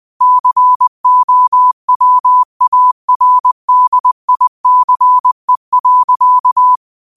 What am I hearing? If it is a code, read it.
Morse code, heard as a keyed tone, about 20 wpm, 1000 Hz. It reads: COWARDICE.
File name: qotdfiles / 20wpm